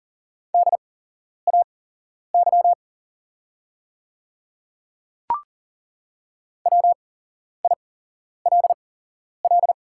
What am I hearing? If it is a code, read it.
Morse code, heard as a keyed tone, about 40 wpm, spaced as for 11 wpm, 700 Hz. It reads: DAY WILL
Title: 8x Character Spacing - Top 100 Words - 40wpm